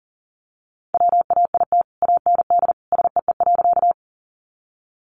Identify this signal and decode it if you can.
Morse code, heard as a keyed tone, about 40 wpm, 700 Hz. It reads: WAIT AND SEE.
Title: Sentences from Top 500 Words 40wpm